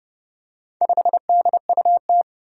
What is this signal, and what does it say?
Morse code, heard as a keyed tone, about 28 wpm, 700 Hz. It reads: 5DUT
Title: Four Challenging Letter-Number Combinations 30wpm